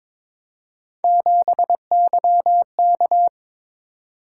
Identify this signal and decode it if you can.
Morse code, heard as a keyed tone, about 27 wpm, 700 Hz. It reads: 7YK
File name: Speed-Racing - Three Letter-Number Combinations - 22wpm